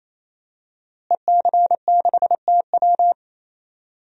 Morse code, heard as a keyed tone, about 27 wpm, 700 Hz. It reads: EC6TW
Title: DXCC-100 Call Signs - 5 Characters - Repeated 3x - 28wpm